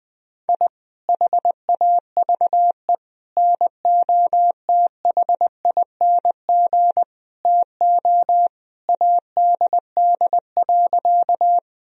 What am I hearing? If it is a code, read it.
Morse code, heard as a keyed tone, about 20 wpm, 700 Hz. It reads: I HAVE NOTHING TO ADD.
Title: Sentences from Top 1000 Words - Review Part 1 - 20wpm